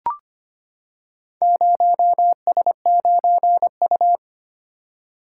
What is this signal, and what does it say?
Morse code, heard as a keyed tone, about 25 wpm, 700 Hz. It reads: 0S9U
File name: practice / 25wpm / Four Letter-Number Combinations - 25wpm